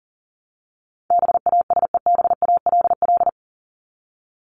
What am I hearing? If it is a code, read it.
Morse code, heard as a keyed tone, about 40 wpm, 700 Hz. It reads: BASEBALL
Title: Top 1751-2000 Words 40wpm